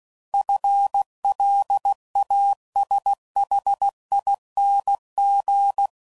Morse code, heard as a keyed tone, about 16 wpm, 800 Hz. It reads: FLASHING
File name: Prefixes / FLs & THs / FL Runs